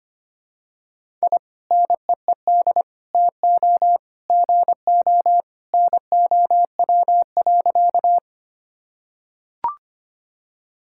Morse code, heard as a keyed tone, about 25 wpm, 700 Hz. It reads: I NEED TO GO NOW.
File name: practice / 25wpm / Sentences with 2-Word Phrases - 25wpm